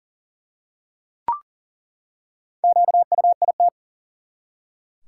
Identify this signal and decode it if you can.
Morse code, heard as a keyed tone, about 40 wpm, 700 Hz. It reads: QUIT
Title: Four Letter Words - 40wpm